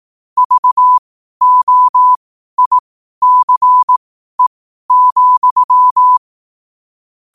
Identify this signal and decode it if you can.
Morse code, heard as a keyed tone, about 18 wpm, spaced as for 12 wpm, 1000 Hz. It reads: VOICE,
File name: qotdfiles / 12wpm